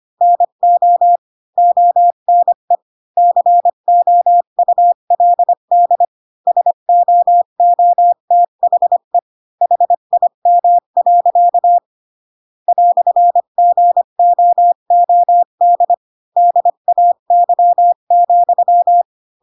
Morse code, heard as a keyed tone, about 25 wpm, 700 Hz. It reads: NO ONE COULD SOOTHE HIM. "GOOD DAY,